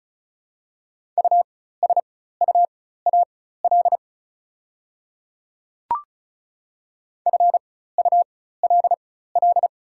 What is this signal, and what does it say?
Morse code, heard as a keyed tone, about 36 wpm, spaced as for 16 wpm, 700 Hz. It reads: USUAL FULL
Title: 4x Character Spacing - Top 500 Words - 35wpm